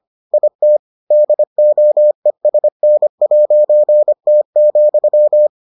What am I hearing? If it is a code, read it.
Morse code, heard as a keyed tone, about 25 wpm, 600 Hz. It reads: IT DOESN'T,